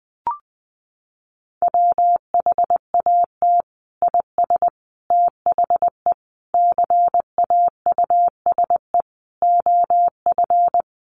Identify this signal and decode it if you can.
Morse code, heard as a keyed tone, about 19 wpm, 700 Hz. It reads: WHAT IS THE CAUSE OF
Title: Sentences from Top 1100 Words 20wpm